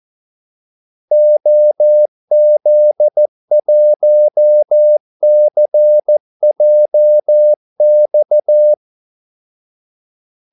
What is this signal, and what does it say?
Morse code, heard as a keyed tone, about 14 wpm, 600 Hz. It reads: OZ1CJX
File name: Callsigns-14wpm